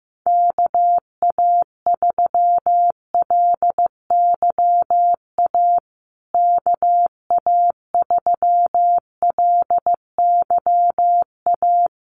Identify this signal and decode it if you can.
Morse code, heard as a keyed tone, about 14 wpm, 700 Hz. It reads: KA3LYA KA3LYA
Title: USA Call Signs Repeated 3x 15wpm